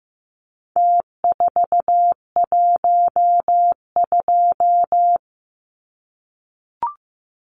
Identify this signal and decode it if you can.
Morse code, heard as a keyed tone, about 15 wpm, 700 Hz. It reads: T412
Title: Numbers - Contest Style - 4-Digit 15wpm